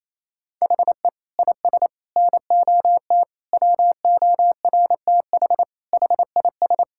Morse code, heard as a keyed tone, about 28 wpm, 700 Hz. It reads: HE IS NOT WORTH HIS
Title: Sentences from Top 1000 Words - Review Part 5 - 28wpm